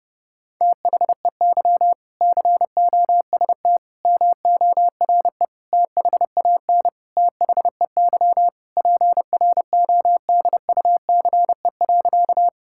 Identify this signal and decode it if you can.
Morse code, heard as a keyed tone, about 30 wpm, 700 Hz. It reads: THEY COST MORE THAN THEY PRODUCE.
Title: Sentences from Top 1000 Words - Review Part 5 - 30wpm